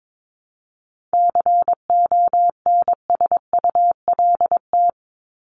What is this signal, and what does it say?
Morse code, heard as a keyed tone, about 22 wpm, 700 Hz. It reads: CONSULT
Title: Top 1751-2000 Words 22wpm